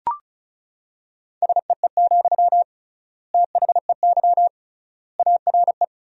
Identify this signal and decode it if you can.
Morse code, heard as a keyed tone, about 36 wpm, 700 Hz. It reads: SEE, THEY ARE
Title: 3x Word-Spacing - Sentences from Top 200 Words - 35wpm